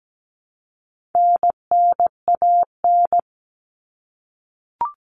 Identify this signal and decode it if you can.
Morse code, heard as a keyed tone, about 17 wpm, 700 Hz. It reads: NNAN
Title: All Cut Numbers - 4-Digit 17wpm